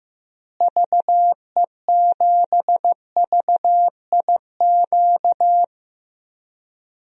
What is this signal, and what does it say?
Morse code, heard as a keyed tone, about 16 wpm, 700 Hz. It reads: VE7VIQ
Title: DXCC-100 Call Signs - 6 Characters - Repeated 2x - 15wpm